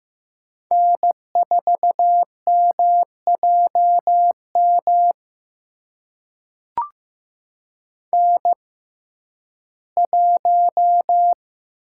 Morse code, heard as a keyed tone, about 15 wpm, spaced as for 11 wpm, 700 Hz. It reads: N4MJM N1
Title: USA Call Signs - 5 Characters - 6x Character Spacing - 15wpm